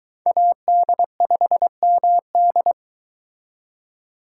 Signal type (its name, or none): Morse code